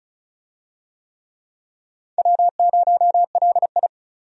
Morse code, heard as a keyed tone, about 36 wpm, 700 Hz. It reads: W0LI
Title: USA Call Signs - 4 Characters - 35wpm